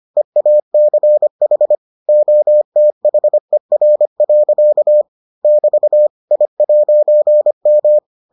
Morse code, heard as a keyed tone, about 25 wpm, 600 Hz. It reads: EACH OTHER. <BT> I'M